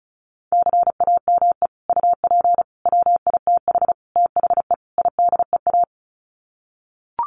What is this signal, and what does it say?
Morse code, heard as a keyed tone, about 34 wpm, 700 Hz. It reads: CAME UP WITH THE IDEA
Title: Sets of 5 Words Encore2 35wpm